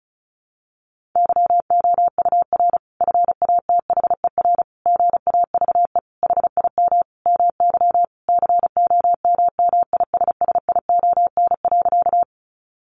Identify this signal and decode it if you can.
Morse code, heard as a keyed tone, about 34 wpm, 700 Hz. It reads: YOUR FATHER GAVE HIM MY COMMISSION.
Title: Sentences from Top 1400 Words 35wpm